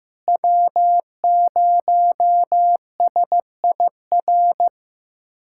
Morse code, heard as a keyed tone, about 15 wpm, 700 Hz. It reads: W0SIR
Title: USA Callsign DE Callsign - 5 Characters - 15wpm